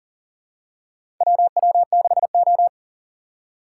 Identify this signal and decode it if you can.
Morse code, heard as a keyed tone, about 40 wpm, 700 Hz. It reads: WW6O
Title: USA Call Signs - 4 Characters - 4x Character Spacing - 40wpm